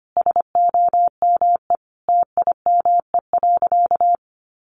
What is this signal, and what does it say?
Morse code, heard as a keyed tone, about 25 wpm, 700 Hz. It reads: SOME TIME.
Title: Sentences from Top 300 Words 25wpm